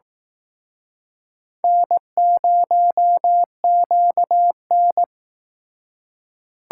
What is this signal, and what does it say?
Morse code, heard as a keyed tone, about 19 wpm, 700 Hz. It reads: N0QN
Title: Speed Racing - USA 1x2 Call Signs 15wpm